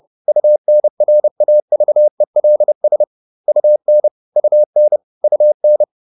Morse code, heard as a keyed tone, about 30 wpm, 600 Hz. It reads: UNRAVELS UN UN UN